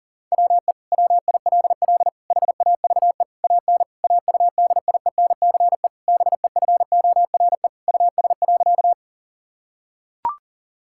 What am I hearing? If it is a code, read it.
Morse code, heard as a keyed tone, about 40 wpm, 700 Hz. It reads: WE WILL HAVE AN AUDIENCE BEFORE US.
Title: Sentences from Top 1000 Words - Review Part 5 - 40wpm